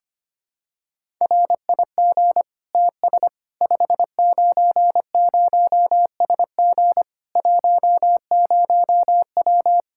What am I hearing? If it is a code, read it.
Morse code, heard as a keyed tone, about 25 wpm, 700 Hz. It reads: RIG TS 590SG 10W
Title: Practice QSO - Exchange 2 - Info - 25wpm